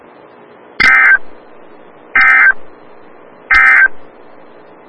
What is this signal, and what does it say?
Emergency Alert System (SAME) tones, heard as harsh buzzing bursts. It is heard from 0.7 to 4.0 s.
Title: Des Moines, IA NOAA Weather Radio - WXL57